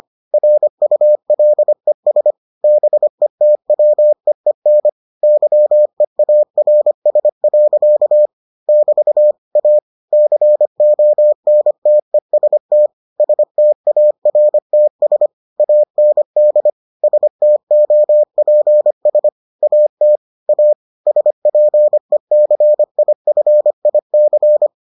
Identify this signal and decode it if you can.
Morse code, heard as a keyed tone, about 25 wpm, 600 Hz. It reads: RULES BETWEEN YEARS. <BT> A CONTEST STARTS AND STOPS AT A SPECIFIC